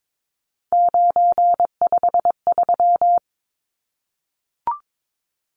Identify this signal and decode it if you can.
Morse code, heard as a keyed tone, about 22 wpm, 700 Hz. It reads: 953
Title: Numbers - 3-Digit 22wpm